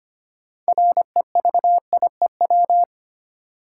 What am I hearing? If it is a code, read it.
Morse code, heard as a keyed tone, about 25 wpm, 700 Hz. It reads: REVIEW
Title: Sentences from Top 900 Words - New Word Drill - 25wpm